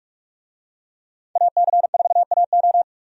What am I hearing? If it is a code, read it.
Morse code, heard as a keyed tone, about 42 wpm, 700 Hz. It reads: AC4AO
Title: USA Call Signs - 5 Characters - 45wpm